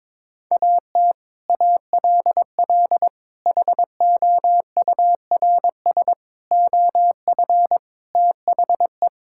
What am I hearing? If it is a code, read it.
Morse code, heard as a keyed tone, about 21 wpm, 700 Hz. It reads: AT ALL HOURS OF THE
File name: practice / 22wpm / Sets of 8 Words - 22wpm